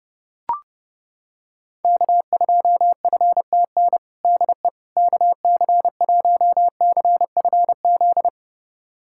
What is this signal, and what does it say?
Morse code, heard as a keyed tone, about 30 wpm, 700 Hz. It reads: K2FTN DE KC1CFZ
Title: USA Callsign DE Callsign 30wpm